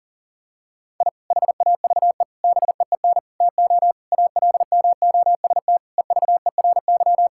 Morse code, heard as a keyed tone, about 40 wpm, 700 Hz. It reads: I HAVE BEEN TO ALMOST EVERY
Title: Sentences from Top 1000 Words - Review Part 5 - 40wpm